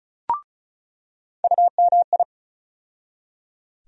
Morse code, heard as a keyed tone, about 36 wpm, 700 Hz. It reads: UMI
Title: USA Call Sign Postfixes - 3 Letters 35wpm